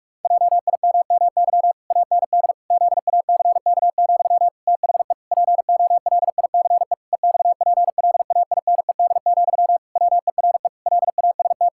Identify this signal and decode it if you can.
Morse code, heard as a keyed tone, about 46 wpm, 700 Hz. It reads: JIMMY AND ZACK, THE POLICE EXPLAINED, WERE LAST
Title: Pangram Challenge v2 - 45wpm